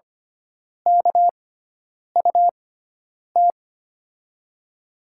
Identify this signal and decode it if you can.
Morse code, heard as a keyed tone, about 25 wpm, spaced as for 9 wpm, 700 Hz. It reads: KUT
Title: USA Call Signs - 6x Character Spacing - 25wpm